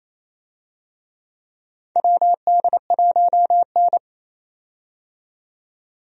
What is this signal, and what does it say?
Morse code, heard as a keyed tone, about 28 wpm, 700 Hz. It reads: WD1N
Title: USA Call Signs - 4 Characters - 28wpm